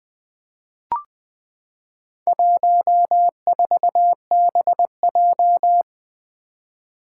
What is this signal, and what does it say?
Morse code, heard as a keyed tone, about 20 wpm, 700 Hz. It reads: 14BJ
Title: Four Challenging Letter-Number Combinations 20wpm